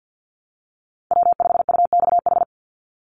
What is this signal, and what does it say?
Morse code, heard as a keyed tone, about 50 wpm, 700 Hz. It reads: W5UXH
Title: Call Signs Repeated 2x - 50wpm